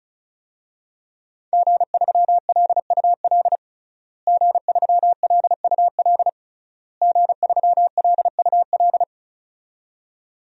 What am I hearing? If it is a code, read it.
Morse code, heard as a keyed tone, about 36 wpm, 700 Hz. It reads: G3LUL G3LUL G3LUL
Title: DXCC-100 Call Signs - 5 Characters - Repeated 3x - 35wpm